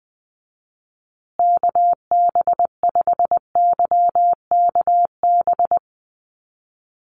Morse code, heard as a keyed tone, about 20 wpm, 700 Hz. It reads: KB5YKB
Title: USA Call Signs Repeated 2x 20wpm